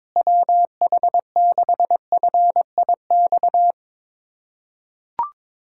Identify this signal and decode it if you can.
Morse code, heard as a keyed tone, about 22 wpm, 700 Hz. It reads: WH6FIX
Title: USA Callsign DE Callsign - 6 Characters - 22wpm